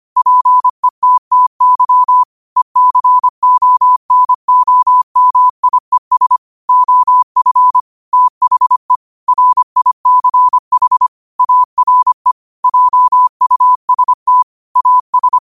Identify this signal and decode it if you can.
Morse code, heard as a keyed tone, about 25 wpm, 1000 Hz. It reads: PETTY ECONOMIES OF THE RICH ARE JUST AS
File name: qotdfiles / 25wpm